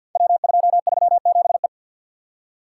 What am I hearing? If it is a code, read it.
Morse code, heard as a keyed tone, about 50 wpm, 700 Hz. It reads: W237E
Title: Code Groups - 5 Letter:Numbers - 50wpm